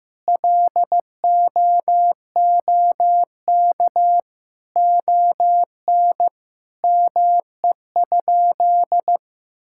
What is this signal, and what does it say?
Morse code, heard as a keyed tone, about 15 wpm, 700 Hz. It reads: LOOK ON ME?
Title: Sentences from Top 1000 Words - Review Part 5 - 15wpm